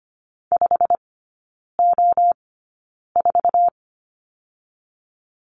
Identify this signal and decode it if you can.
Morse code, heard as a keyed tone, about 25 wpm, spaced as for 9 wpm, 690 Hz. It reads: 5O4
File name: Lesson 15e - Number 5 - Three Characters 25wpm